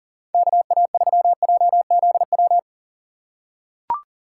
Morse code, heard as a keyed tone, about 40 wpm, 700 Hz. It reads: KA3JZW
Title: USA Call Signs - 6 Characters - 3x Character Spacing - 40wpm